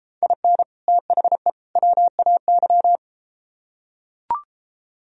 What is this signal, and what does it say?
Morse code, heard as a keyed tone, about 32 wpm, 700 Hz. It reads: IN THE WAY